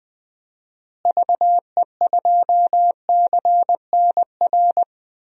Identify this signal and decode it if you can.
Morse code, heard as a keyed tone, about 20 wpm, 700 Hz. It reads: VE2CNR
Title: DXCC-100 Call Signs - 6 Characters - Repeated 3x - 20wpm